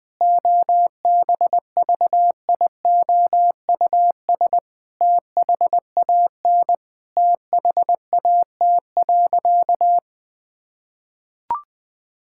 Morse code, heard as a keyed tone, about 20 wpm, 700 Hz. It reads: OBVIOUS THAN THAT.
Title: Sentences from Top 1600 Words 20wpm